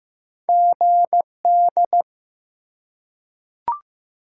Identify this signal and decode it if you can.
Morse code, heard as a keyed tone, about 15 wpm, 700 Hz. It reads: GD
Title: Top 90 QSO Elements 15wpm